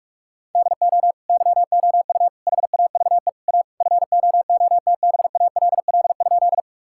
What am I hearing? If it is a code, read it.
Morse code, heard as a keyed tone, about 42 wpm, 700 Hz. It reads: DO YOU HAVE A FOOTBALL?
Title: Sentences from Top 1600 Words 45wpm